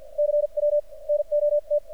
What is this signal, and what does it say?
Morse code, heard as a keyed tone, about 53 wpm, spaced as for 42 wpm, 600 Hz. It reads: CW NOT
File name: archived / CRSnet